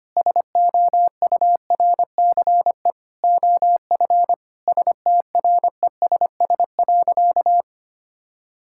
Morse code, heard as a keyed tone, about 25 wpm, 700 Hz. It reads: SOURCE OF STRESS.
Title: Sentences from Top 1000 Words - Review Part 6 - 25wpm